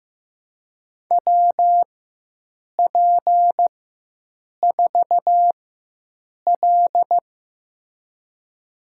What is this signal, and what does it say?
Morse code, heard as a keyed tone, about 15 wpm, spaced as for 7 wpm, 700 Hz. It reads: WP4L
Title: USA Call Signs - 4x Character Spacing - 15wpm